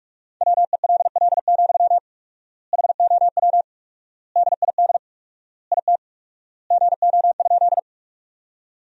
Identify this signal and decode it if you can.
Morse code, heard as a keyed tone, about 46 wpm, 700 Hz. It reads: WELL, HOW DID IT GO?
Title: 4x Word-Spacing - Sentences from Top 200 Words - 45wpm